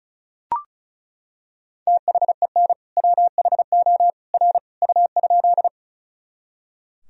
Morse code, heard as a keyed tone, about 34 wpm, 700 Hz. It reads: THEN WHO R U?
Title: Sentences from Top 100 Words and Abbreviations 35wpm